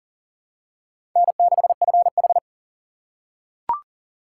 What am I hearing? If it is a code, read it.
Morse code, heard as a keyed tone, about 40 wpm, 700 Hz. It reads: N6FH
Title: USA Call Signs - 2x Character Spacing - 40wpm